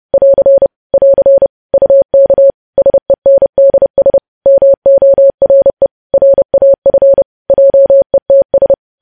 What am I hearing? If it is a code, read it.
Morse code, heard as a keyed tone, about 30 wpm, 550 Hz. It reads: <AR> <AR> UK SENDS MORE RAF JETS
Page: news headlines in morse code at 30 wpm